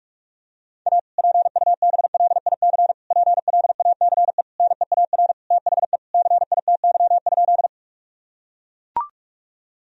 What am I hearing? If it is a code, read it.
Morse code, heard as a keyed tone, about 41 wpm, 700 Hz. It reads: A PUBLIC PLACE NEAR THE CITY?
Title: Sentences from Top 1000 Words - Review Part 3 - 45wpm